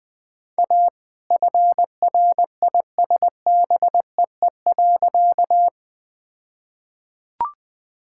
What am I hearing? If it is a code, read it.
Morse code, heard as a keyed tone, about 21 wpm, 700 Hz. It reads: A FRISBEE.
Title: All-Age Appropriate Jokes - 20wpm